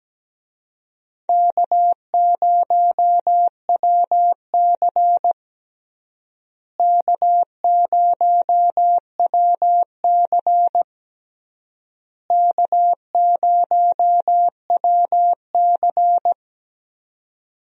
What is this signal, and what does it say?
Morse code, heard as a keyed tone, about 17 wpm, 700 Hz. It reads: K0WC K0WC K0WC
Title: USA Call Signs - 4 Characters - Repeated 3x - 17wpm